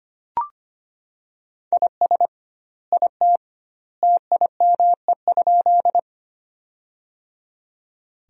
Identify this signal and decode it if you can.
Morse code, heard as a keyed tone, about 25 wpm, 700 Hz. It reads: IS IT TIME?
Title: 2x Word-Spacing - Sentences from Top 100 Words - 25wpm